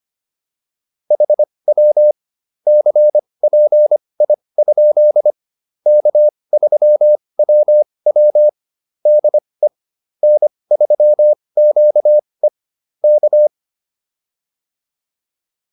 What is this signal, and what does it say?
Morse code, heard as a keyed tone, about 25 wpm, spaced as for 20 wpm, 600 Hz. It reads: HW CPI? K3WW DE N3QE K
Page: advanced-practice-files